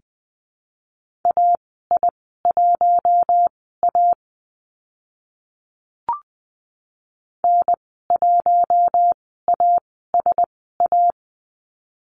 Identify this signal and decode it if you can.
Morse code, heard as a keyed tone, about 20 wpm, spaced as for 15 wpm, 700 Hz. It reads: AI1A N1ASA
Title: 042 Lesson 8g - Number 1 - Call Signs 20wpm